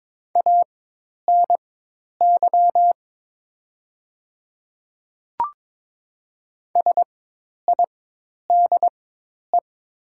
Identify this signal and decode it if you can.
Morse code, heard as a keyed tone, about 22 wpm, spaced as for 10 wpm, 700 Hz. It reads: ANY SIDE
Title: 4x Character Spacing - Top 200 Words - 22wpm